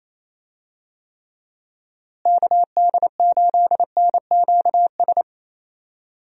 Morse code, heard as a keyed tone, about 28 wpm, 700 Hz. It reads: KD8NQS
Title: USA Call Signs - 6 Characters - Repeated 3x - 28wpm